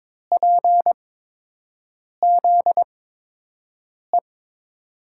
Morse code, heard as a keyed tone, about 22 wpm, spaced as for 6 wpm, 700 Hz. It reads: PZE
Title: USA Call Signs - 8x Character Spacing - 22wpm